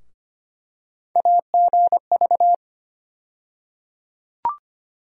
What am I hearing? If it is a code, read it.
Morse code, heard as a keyed tone, about 25 wpm, 700 Hz. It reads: AGV
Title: USA Call Sign Postfixes - 3 Letters 25wpm